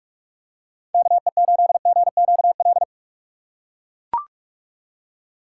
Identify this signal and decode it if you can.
Morse code, heard as a keyed tone, about 40 wpm, 700 Hz. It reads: KE8GQL